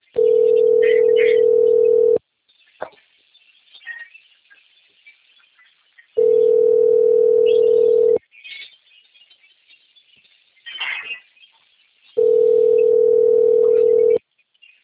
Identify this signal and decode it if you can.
Ringback tone (North America), 440 + 480 Hz.
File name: rg-1600-unknown- - - .29159